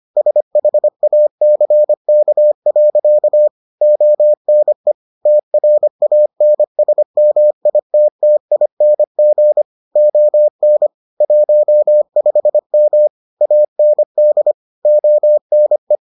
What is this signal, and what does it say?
Morse code, heard as a keyed tone, about 25 wpm, 600 Hz. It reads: SHACK. ONE TRANSMITTING ON 15M AND ONE